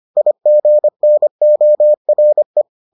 Morse code, heard as a keyed tone, about 25 wpm, 600 Hz. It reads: IGNORE